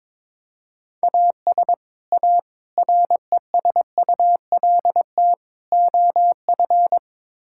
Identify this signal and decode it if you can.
Morse code, heard as a keyed tone, about 22 wpm, 700 Hz. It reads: AS A RESULT OF
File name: practice / 22wpm / Build A Phrase - 5-Word Phrases - 22wpm